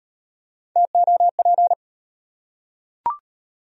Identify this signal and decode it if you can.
Morse code, heard as a keyed tone, about 37 wpm, 700 Hz. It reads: TOP